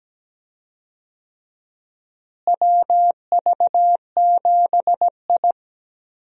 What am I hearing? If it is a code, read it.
Morse code, heard as a keyed tone, about 17 wpm, 700 Hz. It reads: WV7I
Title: USA Call Signs - 4 Characters - Repeated 3x - 17wpm